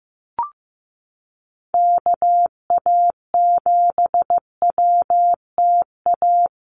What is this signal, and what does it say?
Morse code, heard as a keyed tone, about 15 wpm, 700 Hz. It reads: KA7WTA
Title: USA Call Signs Repeated 3x 15wpm